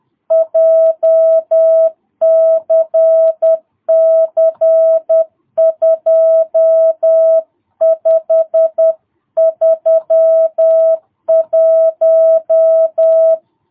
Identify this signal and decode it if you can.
Morse code, heard as a keyed tone, about 10 wpm, 660 Hz. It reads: JCC 2531
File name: jh8cbh / 2022cw